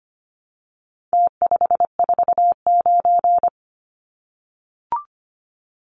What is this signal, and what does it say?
Morse code, heard as a keyed tone, about 25 wpm, 700 Hz. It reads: T549
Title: Numbers - Contest Style - 4-Digit 25wpm